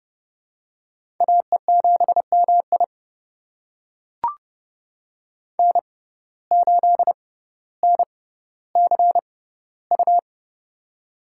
Morse code, heard as a keyed tone, about 30 wpm, spaced as for 15 wpm, 700 Hz. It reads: AE7MI N8NCU
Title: USA Call Signs - 5 Characters - 6x Character Spacing - 30wpm